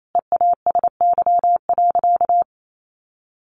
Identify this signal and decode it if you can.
Morse code, heard as a keyed tone, about 28 wpm, 700 Hz. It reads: EASY.
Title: Sentences from Top 700 Words 28wpm